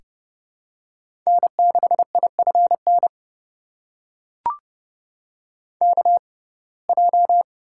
Morse code, heard as a keyed tone, about 30 wpm, spaced as for 22 wpm, 700 Hz. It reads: N6IFN KJ
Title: USA Call Signs - 6x Character Spacing - 30wpm